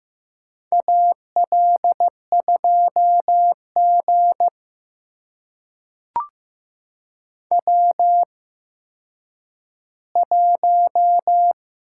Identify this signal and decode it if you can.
Morse code, heard as a keyed tone, about 15 wpm, spaced as for 9 wpm, 700 Hz. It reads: AL2G W1
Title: USA Call Signs - 4 Characters - 8x Character Spacing - 15wpm